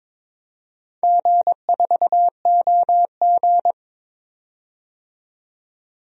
Morse code, heard as a keyed tone, about 21 wpm, 700 Hz. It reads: G4OG
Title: DXCC-100 Call Signs - 4 Characters - 22wpm